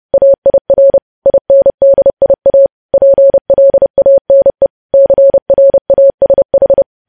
Morse code, heard as a keyed tone, about 30 wpm, 550 Hz. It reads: AIR INDIA PLANE CRASH
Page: news headlines in morse code at 30 wpm